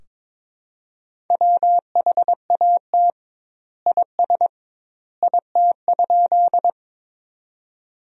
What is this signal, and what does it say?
Morse code, heard as a keyed tone, about 22 wpm, 700 Hz. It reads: WHAT IS IT?
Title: 2x Word-Spacing - Sentences from Top 100 Words - 22wpm